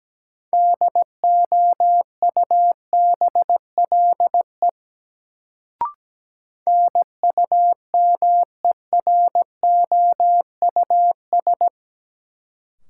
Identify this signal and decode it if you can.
Morse code, heard as a keyed tone, about 17 wpm, 700 Hz. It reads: DOUBLE NUMEROUS
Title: Sentences from Top 1400 Words - New Word Drill Alt - 17wpm